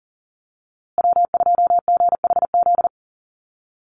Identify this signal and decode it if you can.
Morse code, heard as a keyed tone, about 40 wpm, 700 Hz. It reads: W2GHZ
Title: USA Call Signs Repeated 2x 40wpm